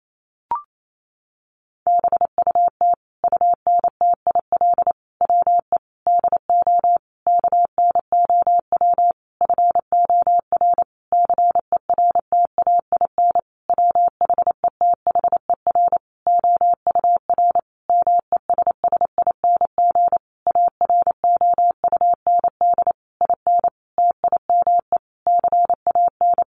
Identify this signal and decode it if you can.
Morse code, heard as a keyed tone, about 28 wpm, 700 Hz. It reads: BUT UNTIL WE DO KNOW FOR CERTAIN WHETHER OUR MESSING AROUND IN TIME CAN
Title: A Sound Of Thunder 28wpm